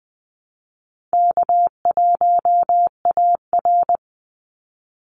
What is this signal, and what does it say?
Morse code, heard as a keyed tone, about 20 wpm, 700 Hz. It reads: K1AR
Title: Call Signs - 20wpm